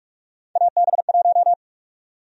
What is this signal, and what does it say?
Morse code, heard as a keyed tone, about 43 wpm, 700 Hz. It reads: AB1
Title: USA Call Sign Prefixes - 45wpm